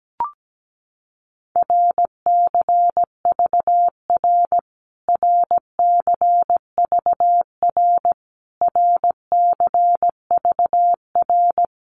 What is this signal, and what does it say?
Morse code, heard as a keyed tone, about 17 wpm, 700 Hz. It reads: RCVR RCVR RCVR
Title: Abbreviations Repeated 3x - 17wpm